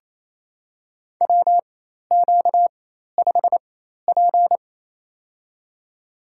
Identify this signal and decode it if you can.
Morse code, heard as a keyed tone, about 28 wpm, spaced as for 13 wpm, 700 Hz. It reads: WQ5P
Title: USA Call Signs - 4 Characters - 4x Character Spacing - 28wpm